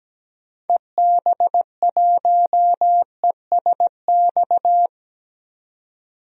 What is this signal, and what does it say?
Morse code, heard as a keyed tone, about 17 wpm, 700 Hz. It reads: EB1ESX
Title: DXCC-100 Call Signs - 6 Characters - 17wpm